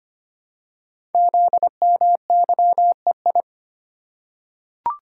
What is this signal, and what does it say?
Morse code, heard as a keyed tone, about 25 wpm, 700 Hz. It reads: ZMYEI
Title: Code Groups - 5 Letters - 25wpm